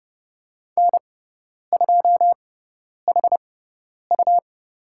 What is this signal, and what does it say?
Morse code, heard as a keyed tone, about 30 wpm, spaced as for 10 wpm, 700 Hz. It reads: N2HU
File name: Lesson 17g - Letter u - Call Signs 30wpm